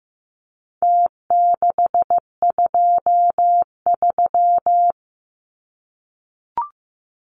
Morse code, heard as a keyed tone, about 15 wpm, 700 Hz. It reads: T623
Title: Numbers - Contest Style - 4-Digit 15wpm